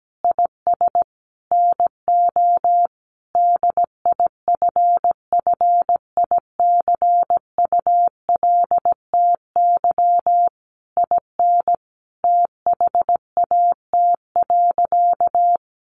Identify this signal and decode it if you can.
Morse code, heard as a keyed tone, about 17 wpm, 700 Hz. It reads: IS NO DIFFICULTY IN THAT.
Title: Sentences from Top 1500 Words 17wpm